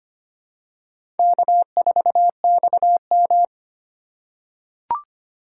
Morse code, heard as a keyed tone, about 25 wpm, 700 Hz. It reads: K4XM
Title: USA Call Signs - 4 Characters - 3x Character Spacing - 25wpm